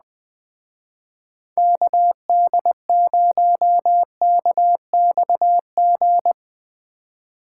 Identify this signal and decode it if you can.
Morse code, heard as a keyed tone, about 20 wpm, 700 Hz. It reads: KD0KXG